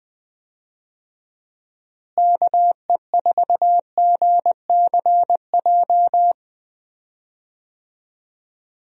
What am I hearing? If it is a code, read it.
Morse code, heard as a keyed tone, about 21 wpm, 700 Hz. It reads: KE4GCJ
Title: USA Call Signs - 6 Characters - Repeated 3x - 20wpm